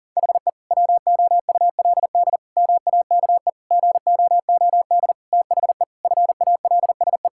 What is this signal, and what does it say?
Morse code, heard as a keyed tone, about 40 wpm, 700 Hz. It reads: HE WOULD MAKE GOOD THE FALSE
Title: Sentences from Top 1600 Words 40wpm